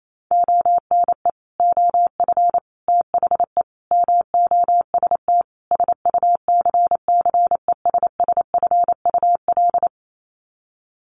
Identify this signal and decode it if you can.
Morse code, heard as a keyed tone, about 28 wpm, 700 Hz. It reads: ONE OF THE MOST SUCCESSFUL
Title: Sets of 5 Words Encore2 28wpm